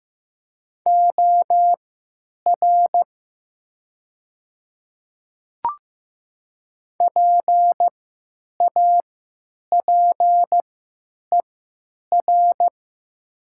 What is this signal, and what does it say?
Morse code, heard as a keyed tone, about 15 wpm, spaced as for 9 wpm, 700 Hz. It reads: OR PAPER
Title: 3x Character Spacing - Top 300 Words - 15wpm